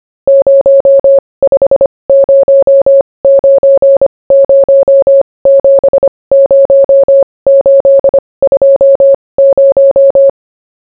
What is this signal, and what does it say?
Morse code, heard as a keyed tone, about 25 wpm, spaced as for 20 wpm, 550 Hz. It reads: 0509070820